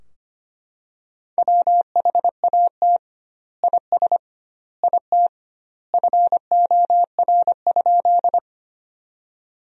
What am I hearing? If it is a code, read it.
Morse code, heard as a keyed tone, about 25 wpm, 700 Hz. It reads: WHAT IS IT FOR?
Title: 2x Word-Spacing - Sentences from Top 100 Words - 25wpm